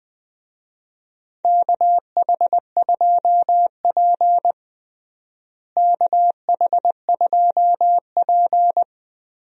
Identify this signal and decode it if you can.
Morse code, heard as a keyed tone, about 20 wpm, 700 Hz. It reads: KH2P KH2P
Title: USA Call Signs - 4 Characters - Repeated 2x - 20wpm